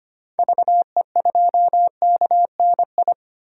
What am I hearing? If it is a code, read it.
Morse code, heard as a keyed tone, about 25 wpm, 700 Hz. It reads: VE2KNI
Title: DXCC-100 Call Signs - 6 Characters - 25wpm